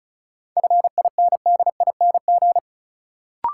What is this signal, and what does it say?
Morse code, heard as a keyed tone, about 34 wpm, 700 Hz. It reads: FINDING
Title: Sentences from Top 1100 Words - New Word Drill Alt - 35wpm